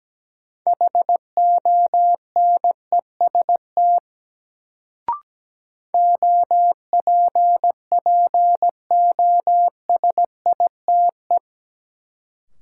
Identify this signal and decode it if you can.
Morse code, heard as a keyed tone, about 17 wpm, 700 Hz. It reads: HONEST OPPOSITE
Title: Sets of 5 Words Encore - New Word Drill - 17wpm